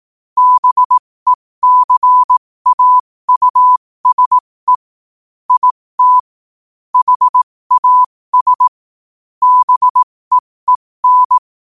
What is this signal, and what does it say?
Morse code, heard as a keyed tone, about 18 wpm, spaced as for 15 wpm, 1000 Hz. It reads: BECAUSE IT HAS BEEN
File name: qotdfiles / 15wpm